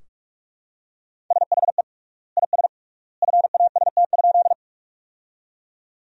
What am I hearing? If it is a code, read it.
Morse code, heard as a keyed tone, about 41 wpm, 700 Hz. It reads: SHE IS FAST?
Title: 3x Word-Spacing - Sentences from Top 400 Words - 45wpm